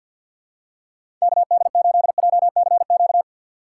Morse code, heard as a keyed tone, about 46 wpm, 700 Hz. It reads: KD8JCQ
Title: USA Call Signs - 6 Characters - 3x Character Spacing - 50wpm